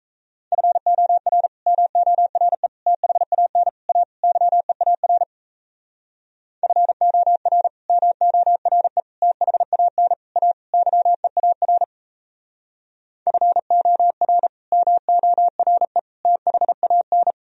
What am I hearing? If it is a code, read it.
Morse code, heard as a keyed tone, about 37 wpm, 700 Hz. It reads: FOR MORE THAN A YEAR FOR MORE THAN A YEAR FOR MORE THAN